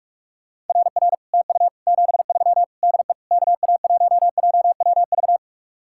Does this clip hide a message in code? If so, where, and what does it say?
Yes: Morse code from 0.7 to 5.4 s, heard as a keyed tone, about 49 wpm, 700 Hz. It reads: RR TU 73 DE KA1JWV